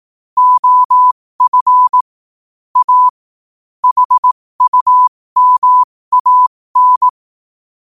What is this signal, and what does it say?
Morse code, heard as a keyed tone, about 18 wpm, spaced as for 15 wpm, 1000 Hz. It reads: OF A HUMAN
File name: qotdfiles / 15wpm